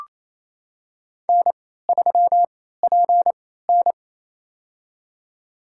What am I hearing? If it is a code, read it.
Morse code, heard as a keyed tone, about 28 wpm, spaced as for 16 wpm, 700 Hz. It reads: N3PN
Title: USA Call Signs - 4 Characters - 3x Character Spacing - 28wpm